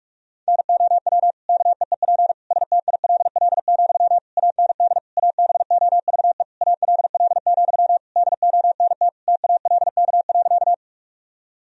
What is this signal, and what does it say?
Morse code, heard as a keyed tone, about 47 wpm, 700 Hz. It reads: NOW KEEP STILL, AND ABOVE ALL, DONT TALK.
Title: Sentences from Top 1000 Words - Review Part 2 - 45wpm